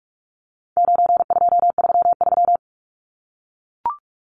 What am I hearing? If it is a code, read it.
Morse code, heard as a keyed tone, about 44 wpm, 700 Hz. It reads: 9233
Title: Numbers - 4-Digit 45wpm